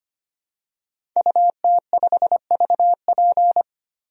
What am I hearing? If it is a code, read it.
Morse code, heard as a keyed tone, about 25 wpm, 700 Hz. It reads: UT5VP
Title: DXCC-100 Call Signs - 5 Characters - Repeated 3x - 25wpm